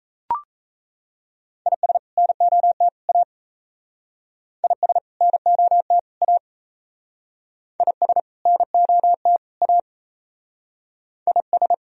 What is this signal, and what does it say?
Morse code, heard as a keyed tone, about 36 wpm, 700 Hz. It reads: IS NOT A IS NOT A IS NOT A IS